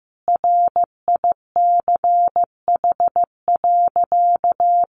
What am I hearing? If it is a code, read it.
Morse code, heard as a keyed tone, about 15 wpm, 700 Hz. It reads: RICH.
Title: Sentences from Top 1100 Words 15wpm